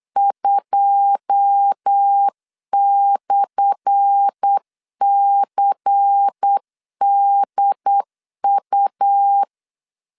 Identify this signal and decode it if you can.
Morse code, heard as a keyed tone, about 8 wpm, 800 Hz. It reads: 2/CDU
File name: LEZIONI / 10 SIGNS / 10C DOT 8wpm 3649